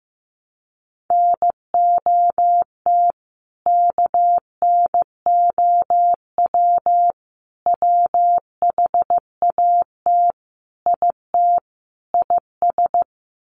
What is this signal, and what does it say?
Morse code, heard as a keyed tone, about 15 wpm, 700 Hz. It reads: NOT KNOW WHAT IT IS
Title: Sets of 5 Words Encore 15wpm